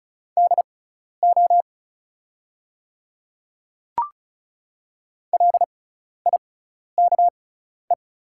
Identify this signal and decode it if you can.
Morse code, heard as a keyed tone, about 36 wpm, spaced as for 12 wpm, 700 Hz. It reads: DO LIKE
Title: 6x Character Spacing - Top 100 Words - 35wpm